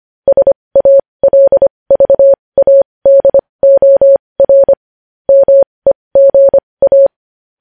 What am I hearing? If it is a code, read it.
Morse code, heard as a keyed tone, about 25 wpm, spaced as for 20 wpm, 550 Hz. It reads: SALVADOR MEGA
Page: news-headlines-in-morse-code-at-20-wpm